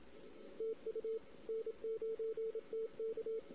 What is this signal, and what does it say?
Morse code, heard as a keyed tone, about 25 wpm, 450 Hz. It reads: TU N9TK
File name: CW / 2008-ss-cw-k5tr-op-n5rz -16:00